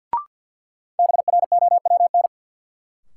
Morse code, heard as a keyed tone, about 51 wpm, 700 Hz. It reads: BROWN
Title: Sentences from Top 1200 Words - New Word Drill - 50wpm